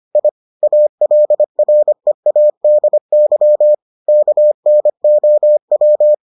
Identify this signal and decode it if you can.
Morse code, heard as a keyed tone, about 25 wpm, 600 Hz. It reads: I ALREADY KNOW